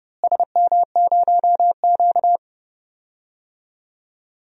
Morse code, heard as a keyed tone, about 31 wpm, 700 Hz. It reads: SM0Q
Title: DXCC-100 Call Signs - 4 Characters - Repeated 2x - 30wpm